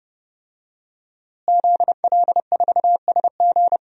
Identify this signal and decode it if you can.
Morse code, heard as a keyed tone, about 30 wpm, 700 Hz. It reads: ZL4SG
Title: DXCC-100 Call Signs - 5 Characters - Repeated 3x - 30wpm